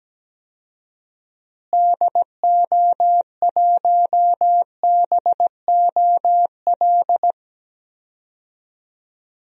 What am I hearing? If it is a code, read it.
Morse code, heard as a keyed tone, about 17 wpm, 700 Hz. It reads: DO1BOL
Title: DXCC-100 Call Signs - 6 Characters - Repeated 2x - 17wpm